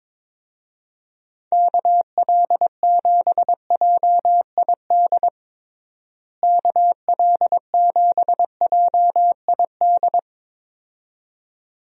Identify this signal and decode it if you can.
Morse code, heard as a keyed tone, about 22 wpm, 700 Hz. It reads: KL7JID KL7JID
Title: DXCC-100 Call Signs - 6 Characters - Repeated 2x - 22wpm